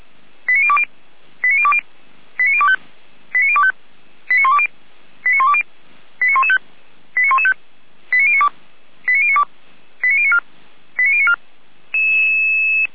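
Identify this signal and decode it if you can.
Five-tone selective call (ZVEI), heard as quick runs of five tones, tones of about 70 ms. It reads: 89020 89020 89025 89025 89120 89120 89115 89115 89092 89092 89094 89094
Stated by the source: ZVEI five-tone selective call 89270 received by SDR.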